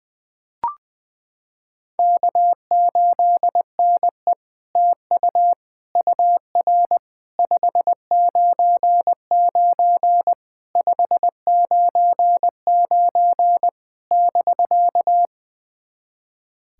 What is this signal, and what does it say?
Morse code, heard as a keyed tone, about 21 wpm, 700 Hz. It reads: K8NE TU UR 599 599 <BK>
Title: SOTA QSO Practice 20wpm